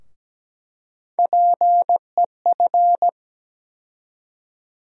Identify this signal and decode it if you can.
Morse code, heard as a keyed tone, about 17 wpm, 700 Hz. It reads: PEF
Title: USA Call Sign Postfixes - 3 Letters 17wpm